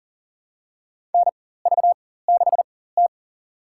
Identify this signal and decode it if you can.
Morse code, heard as a keyed tone, about 40 wpm, spaced as for 19 wpm, 700 Hz. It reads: NV6T
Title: USA Call Signs - 4x Character Spacing - 40wpm